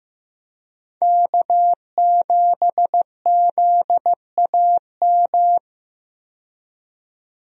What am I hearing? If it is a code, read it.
Morse code, heard as a keyed tone, about 15 wpm, 700 Hz. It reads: K7ZAM
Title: USA Call Signs - 5 Characters - 15wpm